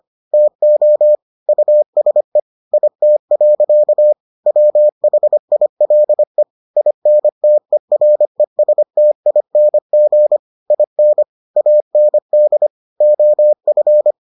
Morse code, heard as a keyed tone, about 25 wpm, 600 Hz. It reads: TO USE IT. WHILE INTERESTING IN AND OF